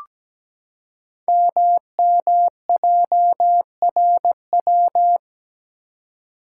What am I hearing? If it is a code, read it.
Morse code, heard as a keyed tone, about 17 wpm, 700 Hz. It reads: MMJRW
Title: Code Groups - 5 Letters - 17wpm